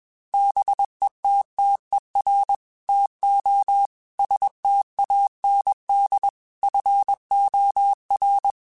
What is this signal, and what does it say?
Morse code, heard as a keyed tone, about 21 wpm, 800 Hz. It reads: BETTER TO STAND FOR
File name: Quotes / Quotes-mp3s / StandFor @20